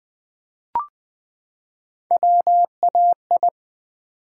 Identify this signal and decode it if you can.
Morse code, heard as a keyed tone, about 20 wpm, 690 Hz. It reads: WAI
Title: Build A Word - 3 Letters At A Time - 6-Letter Words - 20wpm